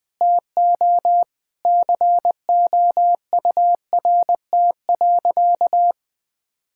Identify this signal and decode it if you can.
Morse code, heard as a keyed tone, about 21 wpm, 700 Hz. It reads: TO COURT.
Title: Sentences from Top 1600 Words 20wpm